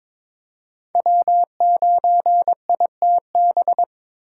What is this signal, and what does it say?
Morse code, heard as a keyed tone, about 21 wpm, 700 Hz. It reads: W9ITB
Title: USA Call Signs - 5 Characters - 4x Character Spacing - 22wpm